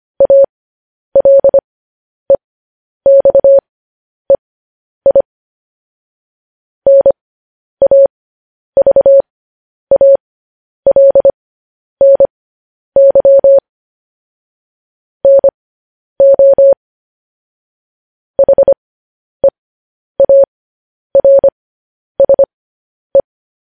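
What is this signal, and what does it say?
Morse code, heard as a keyed tone, about 25 wpm, spaced as for 10 wpm, 550 Hz. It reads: ALEXEI NAVALNY NO HEARSE
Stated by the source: Morse code transmission of news headlines.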